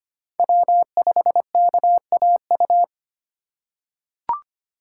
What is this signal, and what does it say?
Morse code, heard as a keyed tone, about 25 wpm, 700 Hz. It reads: W5KAU
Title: USA Call Signs - 5 Characters - 3x Character Spacing - 25wpm